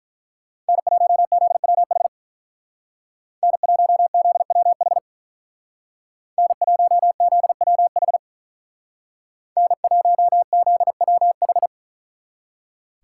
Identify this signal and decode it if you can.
Morse code, heard as a keyed tone, about 44 wpm, 700 Hz. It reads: N1ZWH N1ZWH N1ZWH N1ZWH
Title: Speed Racing - USA Call Signs 35wpm